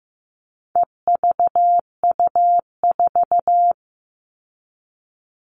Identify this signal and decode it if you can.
Morse code, heard as a keyed tone, about 15 wpm, 700 Hz. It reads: EVU4
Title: All Cut Numbers - 4-Digit 15wpm